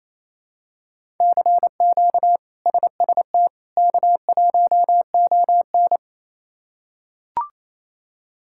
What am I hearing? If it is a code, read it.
Morse code, heard as a keyed tone, about 28 wpm, 700 Hz. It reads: CQ SST K1ON
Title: K1USN SST QSO Practice - 28wpm